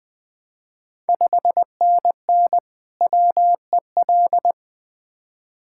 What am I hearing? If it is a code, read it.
Morse code, heard as a keyed tone, about 20 wpm, 690 Hz. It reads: 5NN WEL
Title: Ontario QSO Party Exchange 20wpm